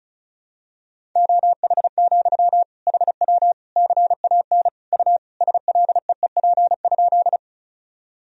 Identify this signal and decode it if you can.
Morse code, heard as a keyed tone, about 36 wpm, 700 Hz. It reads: OH, HW CAN U SLEEP?
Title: Sentences from Top 500 Words and Abbreviations 35wpm